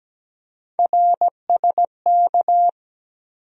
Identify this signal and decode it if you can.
Morse code, heard as a keyed tone, about 17 wpm, 700 Hz. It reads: RSK
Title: USA Call Sign Postfixes - 3 Letters 17wpm